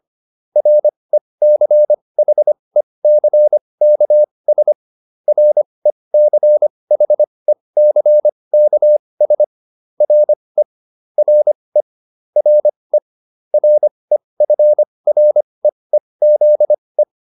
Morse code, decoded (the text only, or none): RECHECKS RECHECKS RE RE RE REFREEZE